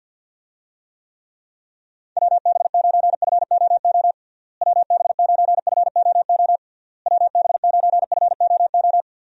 Morse code, heard as a keyed tone, about 47 wpm, 700 Hz. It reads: WB9FOO WB9FOO WB9FOO
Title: USA Call Signs - 6 Characters - Repeated 3x - 50wpm